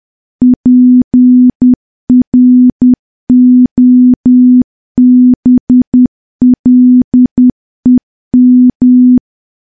Morse code, heard as a keyed tone, about 10 wpm, 250 Hz. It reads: PROBLEM
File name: morse / morse-training-10WPM